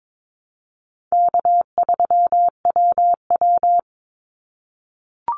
Morse code, heard as a keyed tone, about 22 wpm, 700 Hz. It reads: K3WW
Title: Easy Call Signs - 22wpm